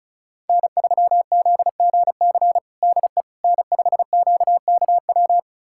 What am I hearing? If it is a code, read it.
Morse code, heard as a keyed tone, about 36 wpm, 700 Hz. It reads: N3ZGC DE N5QKW
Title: USA Callsign DE Callsign - 5 Characters - 35wpm